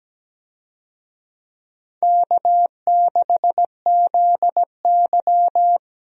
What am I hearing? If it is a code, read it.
Morse code, heard as a keyed tone, about 17 wpm, 700 Hz. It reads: K6ZY
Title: USA Call Signs - 4 Characters - 17wpm